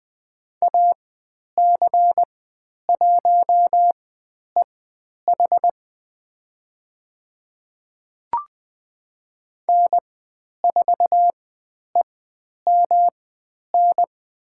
Morse code, heard as a keyed tone, about 20 wpm, spaced as for 10 wpm, 700 Hz. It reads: AC1EH N4EMN
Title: 109 Lesson 18g - Letter m - Call Signs 20wpm